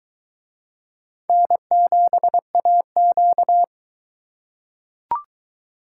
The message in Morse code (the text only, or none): N7AQ